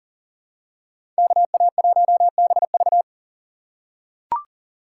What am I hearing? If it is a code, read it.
Morse code, heard as a keyed tone, about 40 wpm, 700 Hz. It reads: KA1BV
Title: USA Call Signs - 5 Characters - 4x Character Spacing - 40wpm